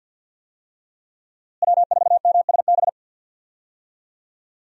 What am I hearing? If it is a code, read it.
Morse code, heard as a keyed tone, about 45 wpm, 700 Hz. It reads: W4MSB
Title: USA Call Signs - 5 Characters - Repeated 2x - 50wpm